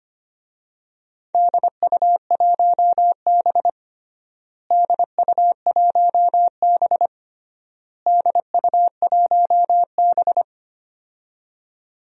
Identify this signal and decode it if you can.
Morse code, heard as a keyed tone, about 25 wpm, 700 Hz. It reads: DU1B DU1B DU1B
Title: DXCC-100 Call Signs - 4 Characters - Repeated 3x - 25wpm